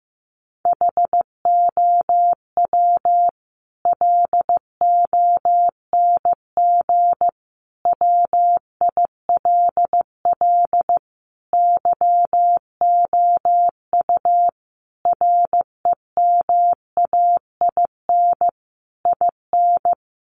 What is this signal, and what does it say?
Morse code, heard as a keyed tone, about 15 wpm, 700 Hz. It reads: HOW LONG WILL YOU REMAIN IN
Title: Sentences from Top 1200 Words 15wpm